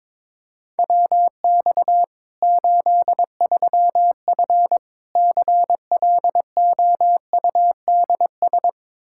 Morse code, heard as a keyed tone, about 22 wpm, 700 Hz. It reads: WX 83F CLOUDS
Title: Practice QSO - Exchange 2 - Info - 22wpm